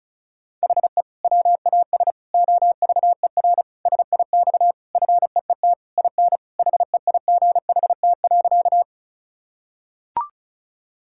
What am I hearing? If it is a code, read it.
Morse code, heard as a keyed tone, about 36 wpm, 700 Hz. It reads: HE WAS OVER SIX FEET IN HEIGHT.
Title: Sentences from Top 1600 Words 35wpm